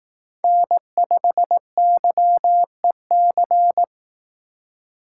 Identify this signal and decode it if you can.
Morse code, heard as a keyed tone, about 19 wpm, 700 Hz. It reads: N5YEC
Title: Speed Racing - USA 1x3 Call Signs 15wpm